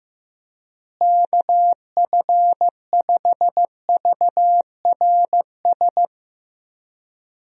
Morse code, heard as a keyed tone, about 16 wpm, 700 Hz. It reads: KF5VRS
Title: USA Call Signs - 6 Characters - Repeated 2x - 15wpm